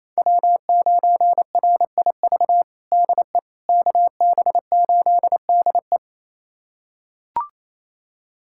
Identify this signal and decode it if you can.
Morse code, heard as a keyed tone, about 27 wpm, 700 Hz. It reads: W9RIV DE KB8DE
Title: USA Callsign DE Callsign - 5 Characters - 28wpm